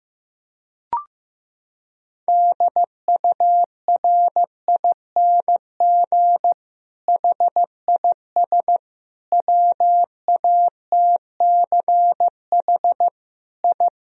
Morse code, heard as a keyed tone, about 15 wpm, 700 Hz. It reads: DURING HIS WATCH I
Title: Sentences from Top 1000 Words - Review Part 2 - 15wpm